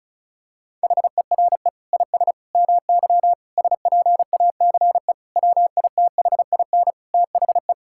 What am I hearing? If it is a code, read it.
Morse code, heard as a keyed tone, about 36 wpm, 700 Hz. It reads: HERE IS MY SPACE WITHIN THE
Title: Sentences from Top 1000 Words - Review Part 5 - 35wpm